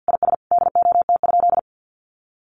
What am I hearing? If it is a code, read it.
Morse code, heard as a keyed tone, about 50 wpm, 700 Hz. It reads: IS DOT?
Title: Sentences from Top 1500 Words 50wpm